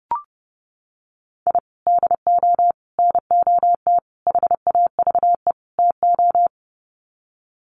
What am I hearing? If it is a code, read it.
Morse code, heard as a keyed tone, about 30 wpm, 700 Hz. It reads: I DO NOT HAVE TO
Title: Sets of 5 Words 30wpm